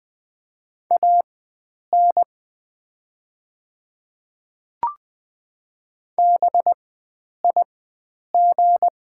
Morse code, heard as a keyed tone, about 20 wpm, spaced as for 9 wpm, 700 Hz. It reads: AN BIG
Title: 4x Character Spacing - Top 500 Words - 20wpm